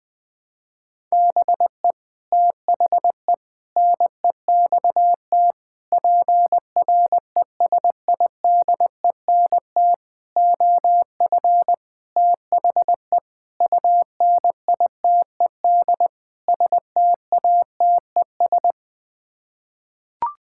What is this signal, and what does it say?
Morse code, heard as a keyed tone, about 20 wpm, 700 Hz. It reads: BE THE NEXT PRESIDENT OF THE UNITED STATES
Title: Sets of 8 Words - 20wpm